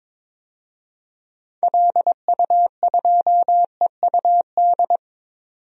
Morse code, heard as a keyed tone, about 22 wpm, 700 Hz. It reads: LU2EUD
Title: DXCC-100 Call Signs - 6 Characters - 22wpm